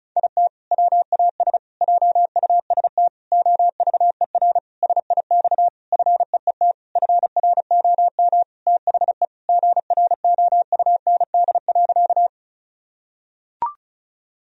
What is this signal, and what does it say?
Morse code, heard as a keyed tone, about 34 wpm, 700 Hz. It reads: IT WAS JUST OVER SIX FEET FROM THE GROUND.
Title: Sentences from Top 1000 Words - Review Part 2 - 35wpm